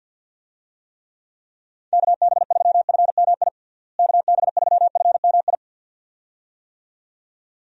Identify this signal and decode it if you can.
Morse code, heard as a keyed tone, about 51 wpm, 700 Hz. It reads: KB3FMI KB3FMI
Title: USA Call Signs - 6 Characters - Repeated 2x - 50wpm